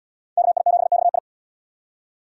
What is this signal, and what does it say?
Morse code, heard as a keyed tone, about 77 wpm, 700 Hz. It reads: KI7XI